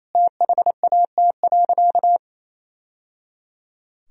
Morse code, heard as a keyed tone, about 28 wpm, 700 Hz. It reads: THAT.
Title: Sentences from Top 1000 Words - Review Part 3 - 28wpm